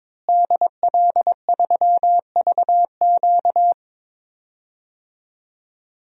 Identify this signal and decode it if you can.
Morse code, heard as a keyed tone, about 22 wpm, 700 Hz. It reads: DL3VQ
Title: DXCC-100 Call Signs - 5 Characters - Repeated 2x - 22wpm